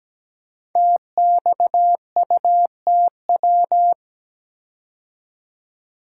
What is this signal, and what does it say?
Morse code, heard as a keyed tone, about 17 wpm, 700 Hz. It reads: TXUTW
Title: Code Groups - 5 Letters - 17wpm